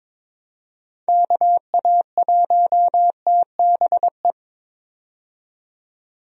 Morse code, heard as a keyed tone, about 22 wpm, 700 Hz. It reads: KA1TBE
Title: USA Call Signs - 6 Characters - 22wpm